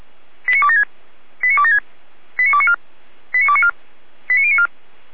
Five-tone selective call (ZVEI), heard as quick runs of five tones, tones of about 70 ms. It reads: 89276 89276 89294 89294 89094